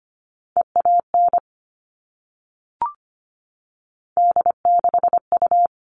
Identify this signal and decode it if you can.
Morse code, heard as a keyed tone, about 25 wpm, 700 Hz. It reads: EAN D6U
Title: All Cut Numbers - 3-Digit 25wpm